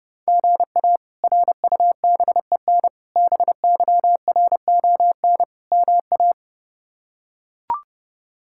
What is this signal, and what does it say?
Morse code, heard as a keyed tone, about 32 wpm, 700 Hz. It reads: GA RUBEN BYRON MA
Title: K1USN SST QSO Practice - 30wpm